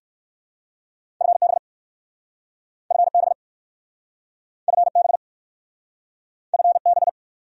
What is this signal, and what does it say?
Morse code, heard as a keyed tone, about 55 wpm, 700 Hz. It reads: FB FB FB FB
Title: Speed-Racing Abbreviations - 45wpm